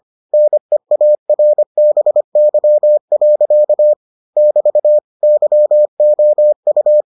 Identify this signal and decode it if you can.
Morse code, heard as a keyed tone, about 25 wpm, 600 Hz. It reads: NEARBY. <BT> YOU